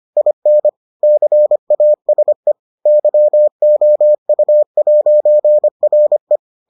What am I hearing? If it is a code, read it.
Morse code, heard as a keyed tone, about 25 wpm, 600 Hz. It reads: IN CASE YOU'RE